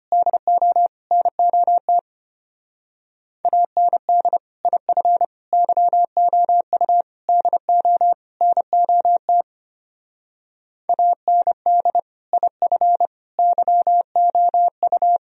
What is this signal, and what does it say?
Morse code, heard as a keyed tone, about 32 wpm, 700 Hz. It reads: DO NOT AND IF YOU DO NOT AND IF YOU